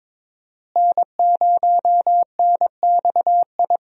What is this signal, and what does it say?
Morse code, heard as a keyed tone, about 21 wpm, 700 Hz. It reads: N0NXI
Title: USA Call Signs - 5 Characters - Repeated 3x - 22wpm